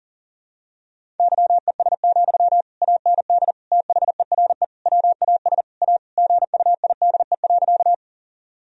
Morse code, heard as a keyed tone, about 40 wpm, 700 Hz. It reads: YES, AND THERE WAS A GUIDE.
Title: Sentences from Top 1000 Words - Review Part 3 - 40wpm